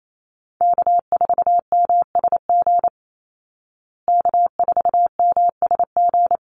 Morse code, heard as a keyed tone, about 28 wpm, 700 Hz. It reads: K4MSG K4MSG
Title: Call Signs Repeated 2x - 28wpm